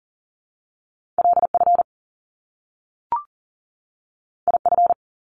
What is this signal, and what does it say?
Morse code, heard as a keyed tone, about 39 wpm, 700 Hz. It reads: LF IF
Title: Top Two Letter Combinations in Words 40wpm